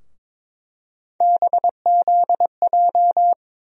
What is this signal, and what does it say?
Morse code, heard as a keyed tone, about 21 wpm, 700 Hz. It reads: BZJ
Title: USA Call Sign Postfixes - 3 Letters 22wpm